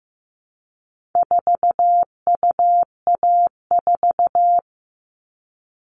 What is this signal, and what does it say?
Morse code, heard as a keyed tone, about 15 wpm, 700 Hz. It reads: 4UA4
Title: All Cut Numbers - 4-Digit 15wpm